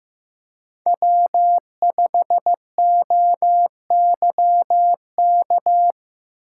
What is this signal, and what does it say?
Morse code, heard as a keyed tone, about 15 wpm, 700 Hz. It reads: W5OYK
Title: USA Call Signs - 5 Characters - 3x Character Spacing - 15wpm